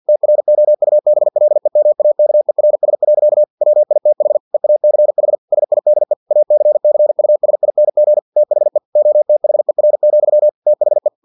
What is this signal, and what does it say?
Morse code, heard as a keyed tone, about 50 wpm, 600 Hz. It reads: TROUBLEMAKERS. WITH EACH SIDE ACCUSING THE OTHER, THE
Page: Never 49wpm